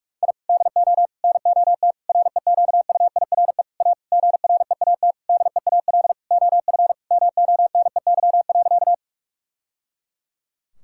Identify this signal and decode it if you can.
Morse code, heard as a keyed tone, about 41 wpm, 700 Hz. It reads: I DO NOT REQUIRE A GREAT DEAL OF MONEY.
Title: Sentences from Top 1000 Words - Review Part 6 - 45wpm